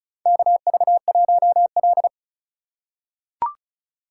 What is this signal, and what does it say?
Morse code, heard as a keyed tone, about 36 wpm, 700 Hz. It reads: KV1L
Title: USA Call Signs - 4 Characters - 2x Character Spacing - 35wpm